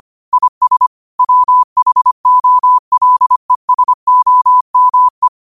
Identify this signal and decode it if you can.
Morse code, heard as a keyed tone, about 25 wpm, 1000 Hz. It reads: IS WHOLESOME
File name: qotdfiles / 25wpm